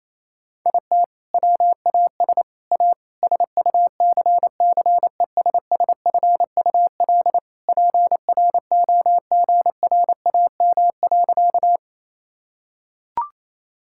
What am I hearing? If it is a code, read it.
Morse code, heard as a keyed tone, about 27 wpm, 700 Hz. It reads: IT WAS A SUCCESSFUL PROGRAM.
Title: Sentences from Top 1000 Words - Review Part 5 - 28wpm